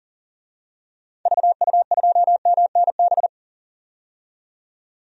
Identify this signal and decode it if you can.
Morse code, heard as a keyed tone, about 40 wpm, 700 Hz. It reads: VU2MNB
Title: DXCC-100 Call Signs - 6 Characters - 40wpm